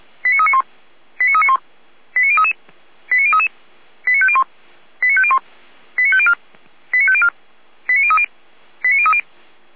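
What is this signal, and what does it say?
Five-tone selective call (ZVEI), heard as quick runs of five tones, tones of about 70 ms. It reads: 89391 89391 89033 89033 89501 89501 89504 89504 89030 89030
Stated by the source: ZVEI five-tone call 89125 received on an SDR.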